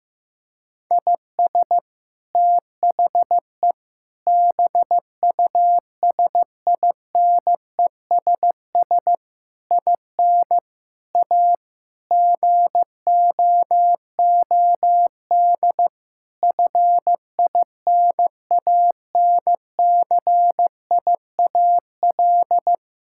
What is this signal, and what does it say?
Morse code, heard as a keyed tone, about 16 wpm, 700 Hz. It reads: IS THE BUSINESS IN A GOOD FINANCIAL
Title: Sentences from Top 1000 Words - Review Part 5 - 15wpm